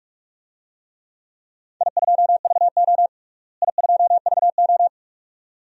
Morse code, heard as a keyed tone, about 42 wpm, 700 Hz. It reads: I2VO I2VO
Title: DXCC-100 Call Signs - 4 Characters - Repeated 2x - 45wpm